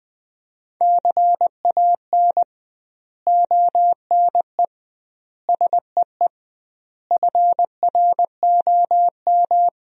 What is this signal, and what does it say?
Morse code, heard as a keyed tone, about 20 wpm, 700 Hz. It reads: CAN ONE SEE FROM
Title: 2x Word-Spacing - Sentences from Top 100 Words - 20wpm